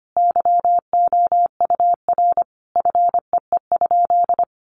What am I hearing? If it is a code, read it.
Morse code, heard as a keyed tone, about 25 wpm, 700 Hz. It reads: YOUR FEE?
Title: Sentences from Top 1500 Words 25wpm